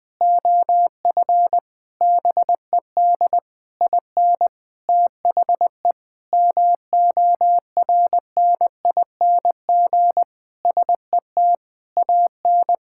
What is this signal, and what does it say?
Morse code, heard as a keyed tone, about 20 wpm, 700 Hz. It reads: OF BED IN THE MORNING SET AN
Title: Sentences with 8-Word Phrases - 20wpm